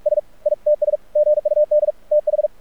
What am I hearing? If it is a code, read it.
Morse code, heard as a keyed tone, about 46 wpm, 600 Hz. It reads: S ITS GUD TH
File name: archived / CRSnet